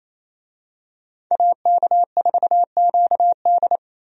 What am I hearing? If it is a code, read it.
Morse code, heard as a keyed tone, about 28 wpm, 700 Hz. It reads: AK4QD
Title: USA Call Signs - 5 Characters - 28wpm